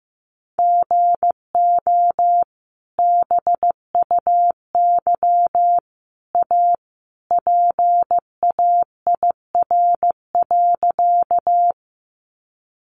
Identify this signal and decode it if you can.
Morse code, heard as a keyed tone, about 15 wpm, 700 Hz. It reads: GO BUY A PAIR.
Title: Sentences from Top 1200 Words 15wpm